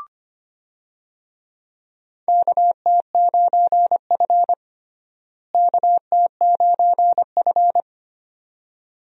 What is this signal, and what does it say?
Morse code, heard as a keyed tone, about 25 wpm, 700 Hz. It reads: KT9F KT9F
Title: USA Call Signs - 4 Characters - Repeated 2x - 25wpm